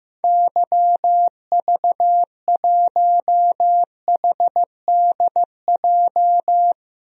Morse code, heard as a keyed tone, about 15 wpm, 700 Hz. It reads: YV1HDJ
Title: DXCC-100 Call Signs - 6 Characters - 15wpm